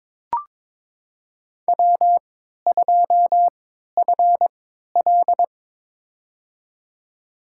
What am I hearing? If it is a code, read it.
Morse code, heard as a keyed tone, about 22 wpm, spaced as for 13 wpm, 700 Hz. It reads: W2FL
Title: USA Call Signs - 4 Characters - 3x Character Spacing - 22wpm